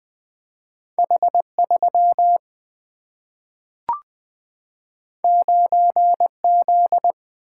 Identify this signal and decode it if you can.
Morse code, heard as a keyed tone, about 20 wpm, 700 Hz. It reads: H3 9Z
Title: Two Letter-Number Combinations - 20wpm